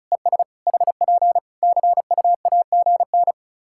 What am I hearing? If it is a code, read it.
Morse code, heard as a keyed tone, about 34 wpm, 700 Hz. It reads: ES HP CUAGN
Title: Practice QSO - Exchange 4 - Ending - 35wpm